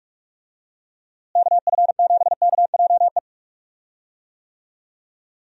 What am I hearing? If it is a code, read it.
Morse code, heard as a keyed tone, about 41 wpm, 700 Hz. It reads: KF7KJE
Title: USA Call Signs - 6 Characters - Repeated 2x - 45wpm